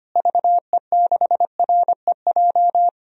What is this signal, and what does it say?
Morse code, heard as a keyed tone, about 25 wpm, 700 Hz. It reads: VE6REJ
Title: DXCC-100 Call Signs - 6 Characters - Repeated 3x - 25wpm